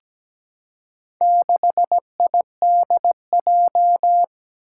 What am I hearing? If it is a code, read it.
Morse code, heard as a keyed tone, about 17 wpm, 700 Hz. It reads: 6IDJ
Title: Four Challenging Letter-Number Combinations 17wpm